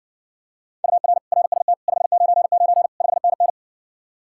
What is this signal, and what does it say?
Morse code, heard as a keyed tone, about 60 wpm, 700 Hz. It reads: UR RST 599 5NN